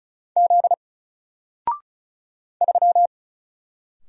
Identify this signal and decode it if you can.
Morse code, heard as a keyed tone, about 36 wpm, 700 Hz. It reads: Z 3
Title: Single Letter-Number Alt 35wpm